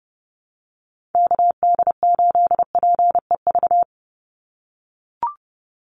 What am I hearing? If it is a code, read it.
Morse code, heard as a keyed tone, about 30 wpm, 700 Hz. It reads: KD8PEV
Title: USA Call Signs 30wpm